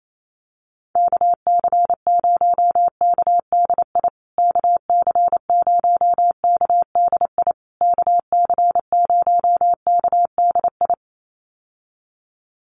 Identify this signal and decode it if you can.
Morse code, heard as a keyed tone, about 28 wpm, 700 Hz. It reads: KC0KDI KC0KDI KC0KDI
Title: USA Call Signs Repeated 3x 28wpm